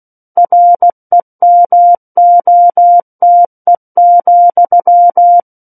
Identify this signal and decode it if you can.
Morse code, heard as a keyed tone, about 16 wpm, 700 Hz. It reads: REMOTE,
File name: Book One / 07-Chapter 7 - 16 WPM